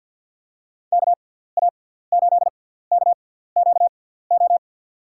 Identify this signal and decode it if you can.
Morse code, heard as a keyed tone, about 49 wpm, spaced as for 17 wpm, 700 Hz. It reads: KA8KQO
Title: USA Call Signs - 6 Characters - 6x Character Spacing - 50wpm